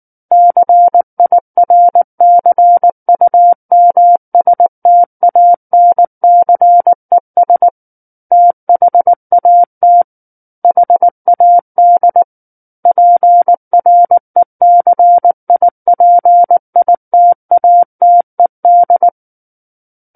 Morse code, heard as a keyed tone, about 19 wpm, 700 Hz. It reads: CIRCUMSTANCES THAT HAD PRECIPITATED
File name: Book One / 10-Chapter 10 - 19 WPM